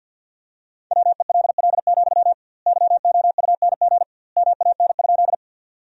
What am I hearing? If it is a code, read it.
Morse code, heard as a keyed tone, about 50 wpm, 700 Hz. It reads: WELL, YOUNG MAN?
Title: 2x Word-Spacing - Sentences from Top 400 Words - 50wpm